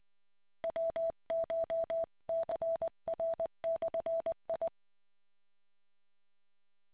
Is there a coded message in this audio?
Yes: Morse code from 0.6 to 4.7 s, heard as a keyed tone, about 20 wpm, 660 Hz. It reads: WCHCR/I